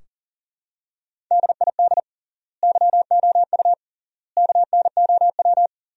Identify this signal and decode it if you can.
Morse code, heard as a keyed tone, about 40 wpm, 700 Hz. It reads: DID YOU KNOW
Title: 3x Word-Spacing - Sentences from Top 100 Words - 40wpm